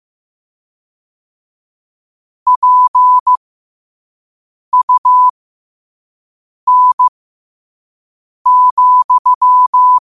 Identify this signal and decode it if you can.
Morse code, heard as a keyed tone, about 15 wpm, spaced as for 5 wpm, 1000 Hz. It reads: PUN,
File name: qotdfiles / 5wpm